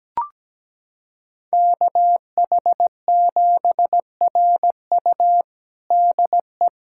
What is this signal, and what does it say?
Morse code, heard as a keyed tone, about 17 wpm, 700 Hz. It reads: KH7RU DE
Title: USA Callsign DE Callsign - 5 Characters - 17wpm